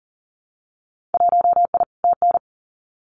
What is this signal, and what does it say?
Morse code, heard as a keyed tone, about 40 wpm, 700 Hz. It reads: 1I TN
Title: Winter Field Day Exchanges 40wpm